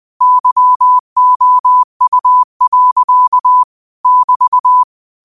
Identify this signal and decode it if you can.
Morse code, heard as a keyed tone, about 20 wpm, 1000 Hz. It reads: YOU. <BT>